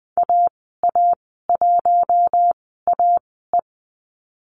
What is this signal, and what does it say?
Morse code, heard as a keyed tone, about 20 wpm, spaced as for 14 wpm, 700 Hz. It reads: AA1AE
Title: 062 Lesson 11g - Letter h - Call Signs 20wpm